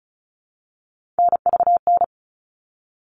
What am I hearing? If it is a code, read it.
Morse code, heard as a keyed tone, about 34 wpm, 700 Hz. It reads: NVN
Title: All Cut Numbers - 3-Digit 35wpm